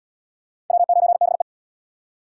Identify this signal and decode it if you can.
Morse code, heard as a keyed tone, about 70 wpm, 700 Hz. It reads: W9BE